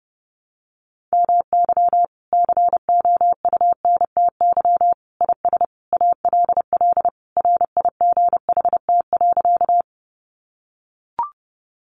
Morse code, heard as a keyed tone, about 30 wpm, 700 Hz. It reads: MY COUNTY IS ALL RIGHT.
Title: Sentences from Top 1200 Words 30wpm